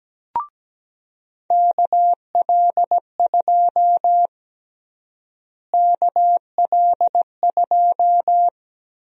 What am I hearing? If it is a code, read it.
Morse code, heard as a keyed tone, about 17 wpm, 700 Hz. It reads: KL2 KL2
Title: USA Call Sign Prefixes - Repeated 2x - 17wpm